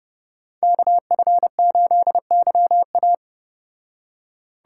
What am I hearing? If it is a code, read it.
Morse code, heard as a keyed tone, about 34 wpm, 700 Hz. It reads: KF8YA